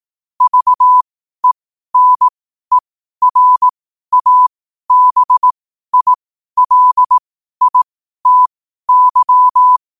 Morse code, heard as a keyed tone, about 18 wpm, spaced as for 12 wpm, 1000 Hz. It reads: VENERABILITY
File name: qotdfiles / 12wpm